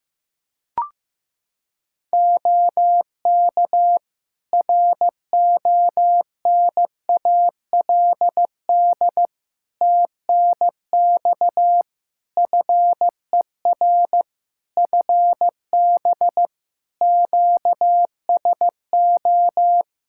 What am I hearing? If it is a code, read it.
Morse code, heard as a keyed tone, about 15 wpm, 700 Hz. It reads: OK RONALD TNX FER FB QSO
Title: Practice QSO - Exchange 4 - Ending - 15wpm